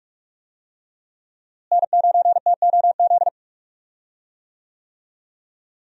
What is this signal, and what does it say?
Morse code, heard as a keyed tone, about 45 wpm, 700 Hz. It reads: N9TOZ
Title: USA Call Signs - 5 Characters - Repeated 3x - 45wpm